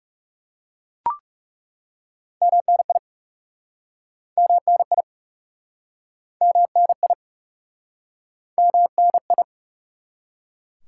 Morse code, heard as a keyed tone, about 40 wpm, 700 Hz. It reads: MNI MNI MNI MNI